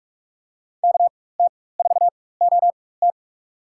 Morse code, heard as a keyed tone, about 46 wpm, spaced as for 21 wpm, 700 Hz. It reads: KT4OT
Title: USA Call Signs - 5 Characters - 4x Character Spacing - 45wpm